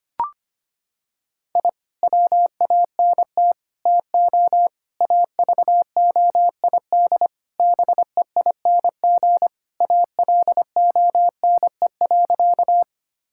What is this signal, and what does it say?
Morse code, heard as a keyed tone, about 25 wpm, 700 Hz. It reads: I WANT TO AVOID BEING ALONE.
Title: Sentences from Top 1000 Words - Review Part 6 - 25wpm